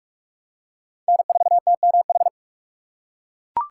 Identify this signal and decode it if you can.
Morse code, heard as a keyed tone, about 40 wpm, 700 Hz. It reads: N4TMH